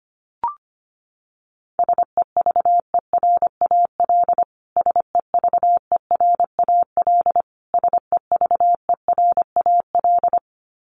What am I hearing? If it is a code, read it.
Morse code, heard as a keyed tone, about 25 wpm, 700 Hz. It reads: SEVERAL SEVERAL SEVERAL
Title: Top 500 Words Repeated 25wpm